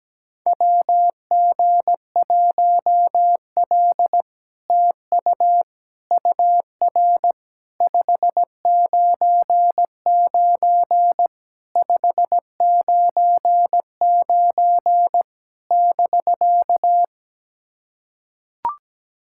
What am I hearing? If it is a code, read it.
Morse code, heard as a keyed tone, about 17 wpm, 700 Hz. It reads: WG1L TU UR 599 599 <BK>
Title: SOTA QSO Practice 17wpm